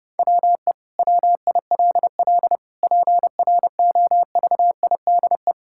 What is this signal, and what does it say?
Morse code, heard as a keyed tone, about 30 wpm, 700 Hz. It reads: WE WILL PROVIDE
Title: Sentences from Top 1000 Words - Review Part 5 - 30wpm